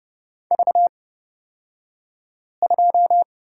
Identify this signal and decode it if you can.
Morse code, heard as a keyed tone, about 30 wpm, spaced as for 5 wpm, 700 Hz. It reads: V2
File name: Lesson 36d - BT - Two Characters 30wpm